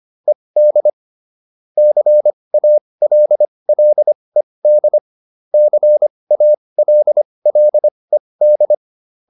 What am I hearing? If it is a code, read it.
Morse code, heard as a keyed tone, about 25 wpm, spaced as for 20 wpm, 600 Hz. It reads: ED CALLED CALLED